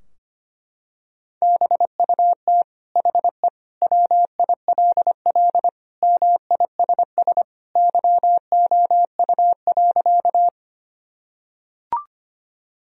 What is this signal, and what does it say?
Morse code, heard as a keyed tone, about 25 wpm, 700 Hz. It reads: BUT HE WILL MISS YOU.
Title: Sentences from Top 1000 Words - Review Part 3 - 25wpm